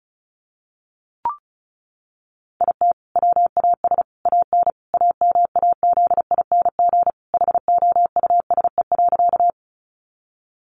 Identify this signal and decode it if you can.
Morse code, heard as a keyed tone, about 34 wpm, 700 Hz. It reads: IT WAS AN AMAZING HOUSE.
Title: Sentences from Top 1500 Words 35wpm